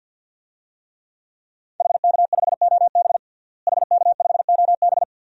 Morse code, heard as a keyed tone, about 52 wpm, 700 Hz. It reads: HK5OB HK5OB
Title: DXCC-100 Call Signs - 5 Characters - Repeated 2x - 50wpm